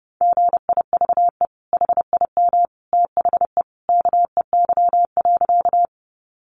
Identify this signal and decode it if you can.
Morse code, heard as a keyed tone, about 30 wpm, 700 Hz. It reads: GIVE HIM THE KEY.
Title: Sentences from Top 600 Words 30wpm